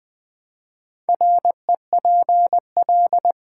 Morse code, heard as a keyed tone, about 20 wpm, 700 Hz. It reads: REPL
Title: Build A Word - 4 Letters At A Time - 11-Letter Words - 20wpm